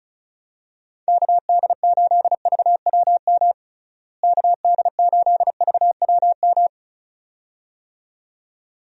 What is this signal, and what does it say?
Morse code, heard as a keyed tone, about 34 wpm, 700 Hz. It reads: KD8VWM KD8VWM
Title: USA Call Signs - 6 Characters - Repeated 2x - 35wpm